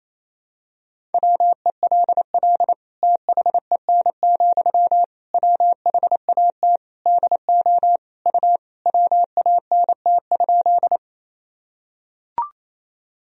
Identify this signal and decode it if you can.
Morse code, heard as a keyed tone, about 27 wpm, 700 Hz. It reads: WELL THEN, WHAT DO U WANT?
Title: Sentences from Top 200 Words and Abbreviations 28wpm